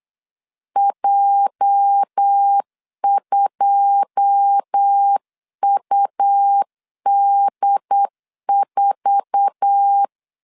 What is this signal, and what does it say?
Morse code, heard as a keyed tone, about 8 wpm, 800 Hz. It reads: J2UD4
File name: 8wpm